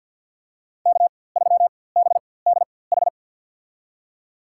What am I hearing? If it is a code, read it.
Morse code, heard as a keyed tone, about 53 wpm, spaced as for 23 wpm, 700 Hz. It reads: K3BDH
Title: USA Call Signs - 5 Characters - 4x Character Spacing - 50wpm